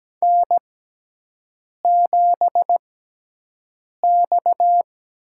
Morse code, heard as a keyed tone, about 17 wpm, spaced as for 6 wpm, 700 Hz. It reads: N7X
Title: USA Call Signs - 5 Characters - 6x Character Spacing - 17wpm